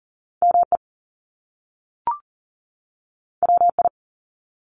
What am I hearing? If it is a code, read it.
Morse code, heard as a keyed tone, about 41 wpm, 700 Hz. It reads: ME WI
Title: US States 40wpm